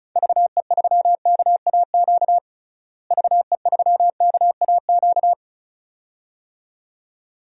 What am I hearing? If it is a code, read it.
Morse code, heard as a keyed tone, about 36 wpm, 700 Hz. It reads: VE3KAQ VE3KAQ
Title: DXCC-100 Call Signs - 6 Characters - Repeated 2x - 35wpm